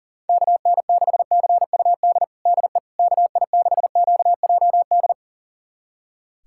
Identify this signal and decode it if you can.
Morse code, heard as a keyed tone, about 40 wpm, 700 Hz. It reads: KN6CUD DE KI6QJD
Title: USA Callsign DE Callsign - 6 Characters - 40wpm